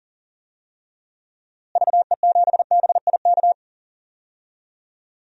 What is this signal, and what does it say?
Morse code, heard as a keyed tone, about 40 wpm, 700 Hz. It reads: VE7BIK
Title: DXCC-100 Call Signs - 6 Characters - Repeated 3x - 40wpm